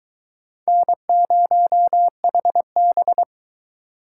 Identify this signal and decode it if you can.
Morse code, heard as a keyed tone, about 22 wpm, 700 Hz. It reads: N0HB